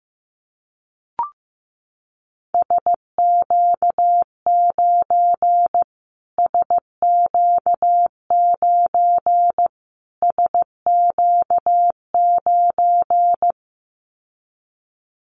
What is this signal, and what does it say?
Morse code, heard as a keyed tone, about 15 wpm, 700 Hz. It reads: SQ9 SQ9 SQ9
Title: Callsign Prefixes for Top 25 DXCC - Repeated 3x - 15wpm